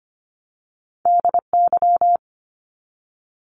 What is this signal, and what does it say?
Morse code, heard as a keyed tone, about 25 wpm, 700 Hz. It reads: DY
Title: Top Two Letter Combinations in Words 25wpm